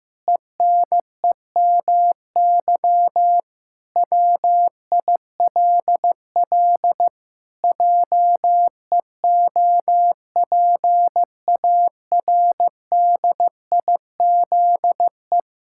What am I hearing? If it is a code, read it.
Morse code, heard as a keyed tone, about 15 wpm, 700 Hz. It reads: ENEMY WILL JEOPARDIZE
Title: Pangram Challenge v2 - 15wpm